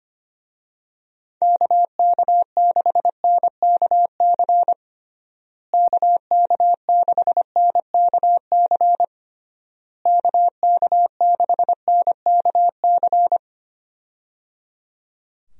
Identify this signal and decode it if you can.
Morse code, heard as a keyed tone, about 25 wpm, 700 Hz. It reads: KK6NKC KK6NKC KK6NKC
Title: USA Call Signs - 6 Characters - Repeated 3x - 25wpm